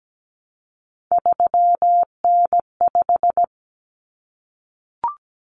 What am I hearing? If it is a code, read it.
Morse code, heard as a keyed tone, about 17 wpm, 700 Hz. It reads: 3N5
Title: Common Cut Numbers - 3-Digit 17wpm